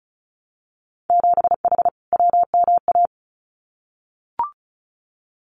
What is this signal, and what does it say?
Morse code, heard as a keyed tone, about 34 wpm, 700 Hz. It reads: 7H WMA
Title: Winter Field Day Exchanges Encore 35wpm